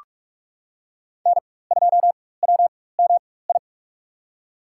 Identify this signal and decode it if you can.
Morse code, heard as a keyed tone, about 42 wpm, spaced as for 21 wpm, 700 Hz. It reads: N2WMI
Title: USA Call Signs - 5 Characters - 4x Character Spacing - 45wpm